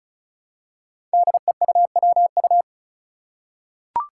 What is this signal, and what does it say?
Morse code, heard as a keyed tone, about 36 wpm, 700 Hz. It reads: DEUWU
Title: Five Challenging Letter-Number Combinations 35wpm